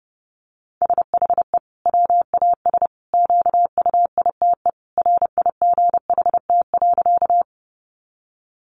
Morse code, heard as a keyed tone, about 30 wpm, 700 Hz. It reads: SHE WAS QUITE RIGHT.
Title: Sentences from Top 700 Words 30wpm